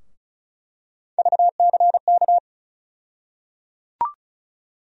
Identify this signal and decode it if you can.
Morse code, heard as a keyed tone, about 36 wpm, 700 Hz. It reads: VCK
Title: USA Call Sign Postfixes - 3 Letters 35wpm